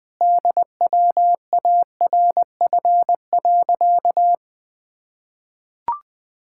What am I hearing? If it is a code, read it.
Morse code, heard as a keyed tone, about 21 wpm, 700 Hz. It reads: DWARF.
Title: Pangram Challenge v2 - 20wpm